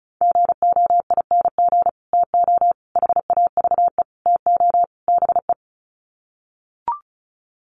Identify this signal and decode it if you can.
Morse code, heard as a keyed tone, about 34 wpm, 700 Hz. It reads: GOING TO HAVE TO BE
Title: Sets of 5 Words 35wpm